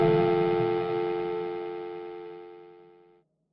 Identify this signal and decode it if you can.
Dial tone (North America), 350 + 440 Hz.